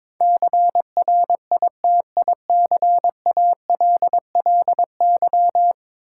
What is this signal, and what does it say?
Morse code, heard as a keyed tone, about 22 wpm, 700 Hz. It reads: CRITICALLY
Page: morse-code-ninja-practice-22wpm